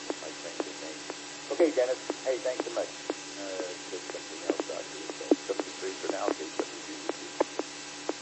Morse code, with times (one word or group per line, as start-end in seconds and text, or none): none